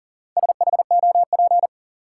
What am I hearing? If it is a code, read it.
Morse code, heard as a keyed tone, about 40 wpm, 700 Hz. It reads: SHOP
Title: Sentences from Top 1600 Words - New Word Drill - 40wpm